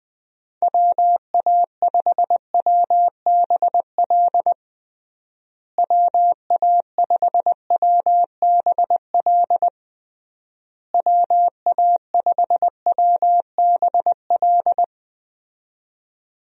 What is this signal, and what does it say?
Morse code, heard as a keyed tone, about 21 wpm, 700 Hz. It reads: WA5WBL WA5WBL WA5WBL
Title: USA Call Signs - 6 Characters - Repeated 3x - 20wpm